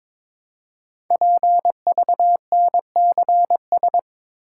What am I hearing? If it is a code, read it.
Morse code, heard as a keyed tone, about 22 wpm, 700 Hz. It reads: PVNCS
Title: Code Groups - 5 Letter:Numbers - 22wpm